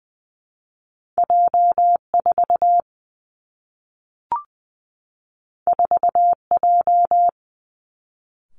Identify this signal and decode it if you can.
Morse code, heard as a keyed tone, about 20 wpm, 700 Hz. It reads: J 4 4 J
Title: Two Challenging Letter-Number Combinations 20wpm